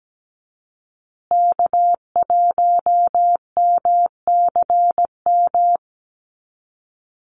Morse code, heard as a keyed tone, about 17 wpm, 700 Hz. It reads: K1MCM
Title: USA Call Signs Repeated 3x 17wpm